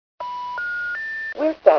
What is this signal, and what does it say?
Special information tone, with speech.